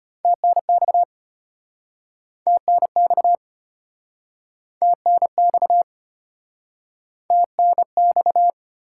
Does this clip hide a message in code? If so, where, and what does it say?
Yes: Morse code from 0.2 to 8.5 s, heard as a keyed tone, about 35 wpm, 700 Hz. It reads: TNX TNX TNX TNX